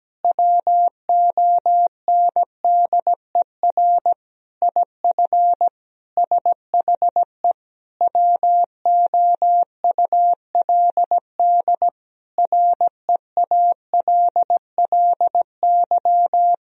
Morse code, heard as a keyed tone, about 17 wpm, 700 Hz. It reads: WONDER IF SHE WOULD REALLY
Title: Sentences from Top 1000 Words - Review Part 3 - 17wpm